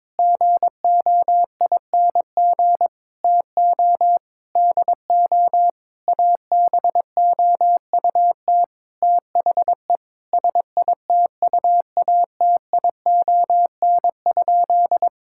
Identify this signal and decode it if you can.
Morse code, heard as a keyed tone, about 22 wpm, 700 Hz. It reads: GOING TO DO ABOUT THE SITUATION?
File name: practice / 22wpm / Sentences with 8-Word Phrases - 22wpm